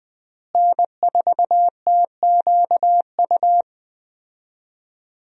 Morse code, heard as a keyed tone, about 20 wpm, 700 Hz. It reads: N4TQU
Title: USA Call Signs - 5 Characters - Repeated 2x - 20wpm